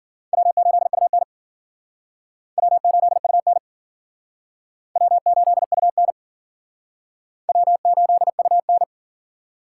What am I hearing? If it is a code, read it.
Morse code, heard as a keyed tone, about 54 wpm, 700 Hz. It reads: W8UN W8UN W8UN W8UN